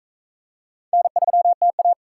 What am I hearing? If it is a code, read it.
Morse code, heard as a keyed tone, about 44 wpm, 700 Hz. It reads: N3TA